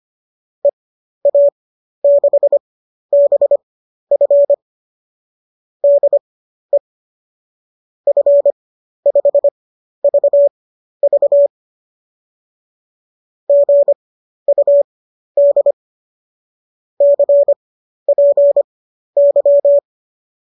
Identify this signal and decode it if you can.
Morse code, heard as a keyed tone, about 25 wpm, spaced as for 12 wpm, 570 Hz. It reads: EA6BF DE F5VV GUD CPY
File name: qso009 12